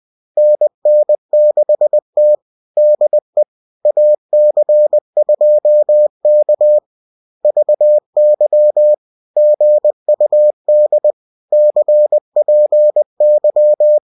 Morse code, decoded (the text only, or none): NN6T DE AC2K VY GUD CPY